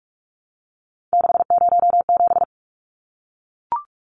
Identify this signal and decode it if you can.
Morse code, heard as a keyed tone, about 44 wpm, 700 Hz. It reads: 607
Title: Numbers - 3-Digit 45wpm